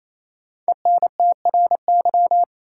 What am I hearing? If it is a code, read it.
Morse code, heard as a keyed tone, about 28 wpm, 700 Hz. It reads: ENTRY
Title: Sentences from Top 1500 Words - New Word Drill - 28wpm